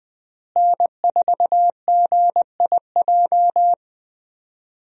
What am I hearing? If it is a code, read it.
Morse code, heard as a keyed tone, about 20 wpm, 700 Hz. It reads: N4GIJ
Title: USA Call Signs - 5 Characters - Repeated 2x - 20wpm